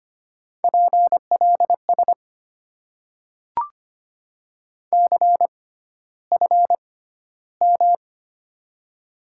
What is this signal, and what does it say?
Morse code, heard as a keyed tone, about 25 wpm, spaced as for 13 wpm, 700 Hz. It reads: PLS CFM
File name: Top 90 QSO Elements - 6x Character Spacing - 25wpm